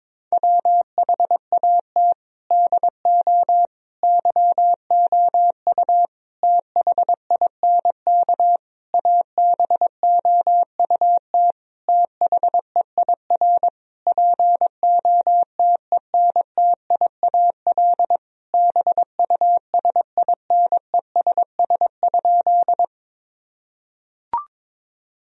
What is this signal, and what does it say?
Morse code, heard as a keyed tone, about 22 wpm, 700 Hz. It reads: WHAT DO YOU THINK ABOUT THEIR POTENTIAL BUSINESS?
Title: Sentences from Top 1000 Words - Review Part 4 - 22wpm